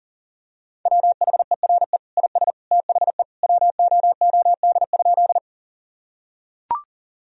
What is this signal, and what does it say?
Morse code, heard as a keyed tone, about 40 wpm, 700 Hz. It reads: WHERE IS THE WOOD?
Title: Sentences from Top 1000 Words - Review Part 2 - 40wpm